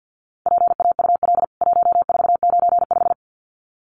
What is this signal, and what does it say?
Morse code, heard as a keyed tone, about 49 wpm, 700 Hz. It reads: PAUL 1495
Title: CWT Exchange 50wpm